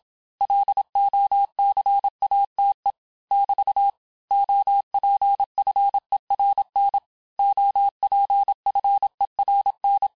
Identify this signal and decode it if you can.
Morse code, heard as a keyed tone, about 27 wpm, 800 Hz. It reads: LOCATE <BT> OPFERN OPFERN
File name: cours-du-mercredi / audios / cours-869-texte-LES-MOTS-vitesse-25-MPM